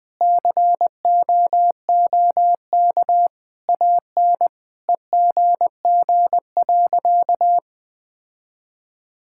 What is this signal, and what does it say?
Morse code, heard as a keyed tone, about 20 wpm, 700 Hz. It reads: COOK AN EGG.
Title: Sentences with 7-Word Phrases - 20wpm